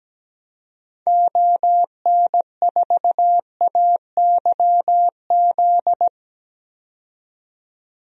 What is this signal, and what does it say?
Morse code, heard as a keyed tone, about 17 wpm, 700 Hz. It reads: ON4AYZ
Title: DXCC-100 Call Signs - 6 Characters - 17wpm